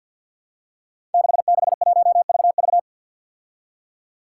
Morse code, heard as a keyed tone, about 51 wpm, 700 Hz. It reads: 661VV
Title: Five Challenging Letter-Number Combinations 50wpm